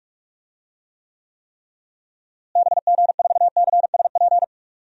Morse code, heard as a keyed tone, about 46 wpm, 700 Hz. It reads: BG4CSP
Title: DXCC-100 Call Signs - 6 Characters - Repeated 3x - 45wpm